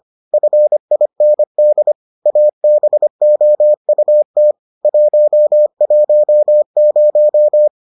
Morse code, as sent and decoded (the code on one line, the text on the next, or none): ..-. .. -. -.. / .- -... --- ..- - / .---- .---- -----
FIND ABOUT 110